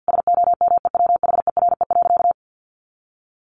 Morse code, heard as a keyed tone, about 50 wpm, 700 Hz. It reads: SOMEWHERE.
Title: Sentences from Top 1400 Words 50wpm